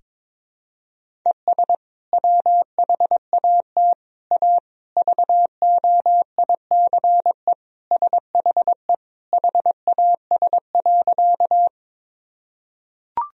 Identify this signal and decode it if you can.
Morse code, heard as a keyed tone, about 21 wpm, 700 Hz. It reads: ES WHAT A VOICE SHE HAS.
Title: Sentences from Top 500 Words and Abbreviations 22wpm